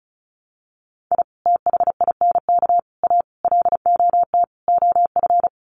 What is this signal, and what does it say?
Morse code, heard as a keyed tone, about 34 wpm, 700 Hz. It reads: I THINK A LOT OF
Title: Sets of 5 Words 35wpm